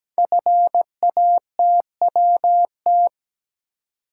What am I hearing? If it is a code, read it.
Morse code, heard as a keyed tone, about 17 wpm, 700 Hz. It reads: FATWT
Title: Code Groups - 5 Letters - 17wpm